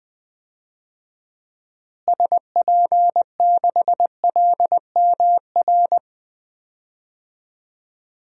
Morse code, heard as a keyed tone, about 21 wpm, 700 Hz. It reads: SP6LMR
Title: DXCC-100 Call Signs - 6 Characters - Repeated 2x - 20wpm